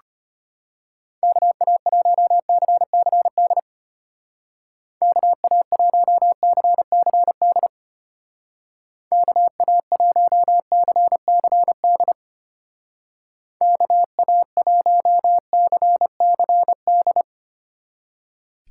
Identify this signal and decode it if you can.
Morse code, heard as a keyed tone, about 31 wpm, 700 Hz. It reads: KA1CCD KA1CCD KA1CCD KA1CCD